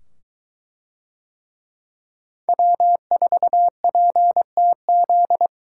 Morse code, heard as a keyed tone, about 22 wpm, 700 Hz. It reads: W4PTZ